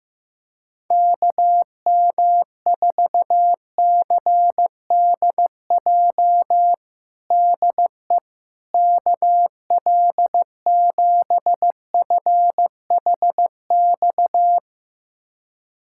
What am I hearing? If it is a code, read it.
Morse code, heard as a keyed tone, about 15 wpm, 700 Hz. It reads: KM4CDJ DE KL7FHX
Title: USA Callsign DE Callsign - 6 Characters - 15wpm